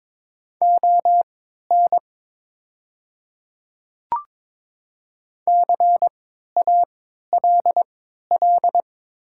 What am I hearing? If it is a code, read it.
Morse code, heard as a keyed tone, about 22 wpm, spaced as for 13 wpm, 700 Hz. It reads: ON CALL
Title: 3x Character Spacing - Top 100 Words - 22wpm